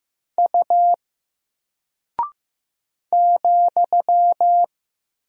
Morse code, heard as a keyed tone, about 15 wpm, 700 Hz. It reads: U ,
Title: Single Character (Letter:Number:Punctuation:Prosign) Alt 15wpm